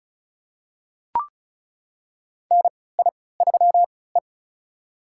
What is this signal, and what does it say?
Morse code, heard as a keyed tone, about 36 wpm, spaced as for 20 wpm, 700 Hz. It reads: NI3E
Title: USA Call Signs - 3x Character Spacing - 35wpm